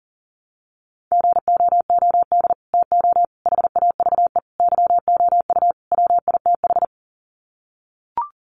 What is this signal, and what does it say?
Morse code, heard as a keyed tone, about 40 wpm, 700 Hz. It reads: GOOD TO HAVE YOU WITH
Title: Sets of 5 Words Encore2 40wpm